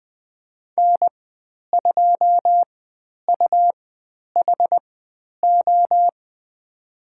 Morse code, heard as a keyed tone, about 20 wpm, spaced as for 10 wpm, 700 Hz. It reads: N2UHO
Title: 102 Lesson 17g - Letter u - Call Signs 20wpm